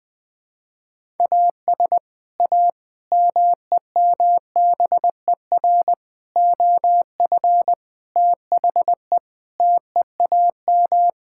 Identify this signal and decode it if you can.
Morse code, heard as a keyed tone, about 20 wpm, 700 Hz. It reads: AS A MEMBER OF THE TEAM
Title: Sentences with 5-Word Phrases - Spoken Beginning - 20wpm